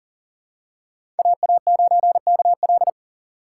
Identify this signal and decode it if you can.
Morse code, heard as a keyed tone, about 38 wpm, 700 Hz. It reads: AA9KL
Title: USA Call Signs - 5 Characters - 2x Character Spacing - 40wpm